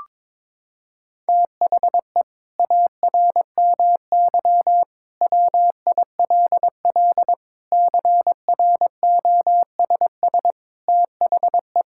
Morse code, heard as a keyed tone, about 22 wpm, 700 Hz. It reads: THE ARMY WILL CROSS THE
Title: Sentences from Top 1000 Words - Review Part 5 - 22wpm